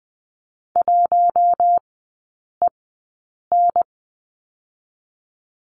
Morse code, heard as a keyed tone, about 20 wpm, spaced as for 8 wpm, 700 Hz. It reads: 1EN
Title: 041 Lesson 8e - Number 1 - Three Characters 20wpm